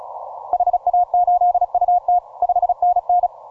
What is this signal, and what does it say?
Morse code, heard as a keyed tone, about 33 wpm, 710 Hz. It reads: HA8UT 5NN